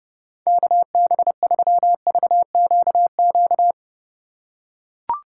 Morse code, heard as a keyed tone, about 30 wpm, 700 Hz. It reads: KB3VQQ
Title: USA Call Signs - 6 Characters - 4x Character Spacing - 30wpm